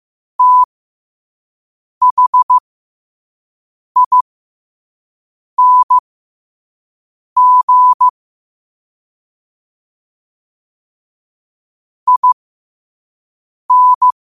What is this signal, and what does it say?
Morse code, heard as a keyed tone, about 15 wpm, spaced as for 5 wpm, 1000 Hz. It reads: THING IN
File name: qotdfiles / 5wpm